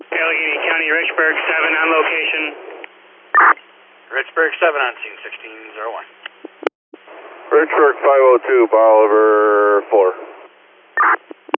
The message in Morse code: none

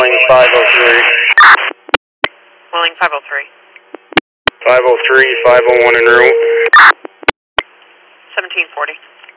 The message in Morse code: none

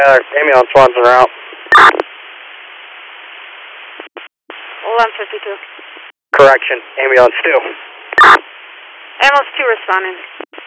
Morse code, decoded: none

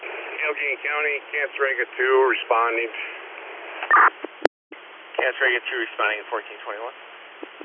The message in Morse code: none

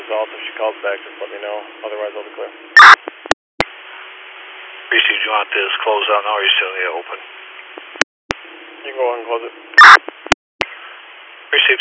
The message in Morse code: none

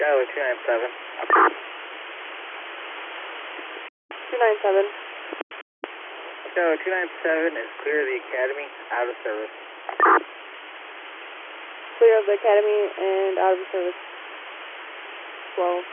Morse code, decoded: none

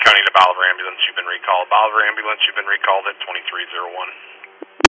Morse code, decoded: none

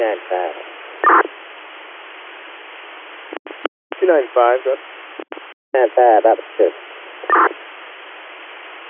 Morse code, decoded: none

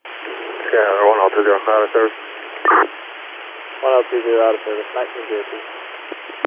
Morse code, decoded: none

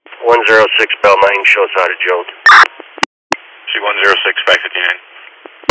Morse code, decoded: none

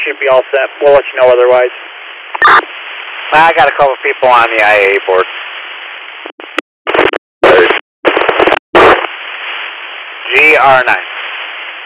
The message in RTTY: none

